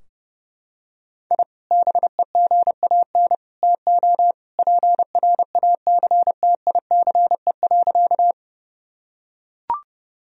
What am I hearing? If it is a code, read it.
Morse code, heard as a keyed tone, about 29 wpm, 700 Hz. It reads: I BEGAN TO PRACTICE.
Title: Sentences from Top 1000 Words - Review Part 4 - 30wpm